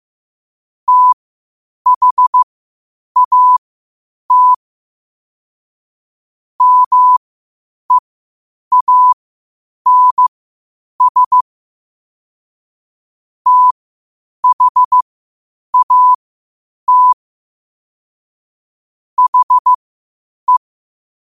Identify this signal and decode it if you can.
Morse code, heard as a keyed tone, about 15 wpm, spaced as for 8 wpm, 1000 Hz. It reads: THAT MEANS THAT HE
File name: qotdfiles / 8wpm